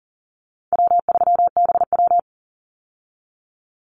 Morse code, heard as a keyed tone, about 40 wpm, 700 Hz. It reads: W3BW
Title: Easy Call Signs Repeated 3x - 40wpm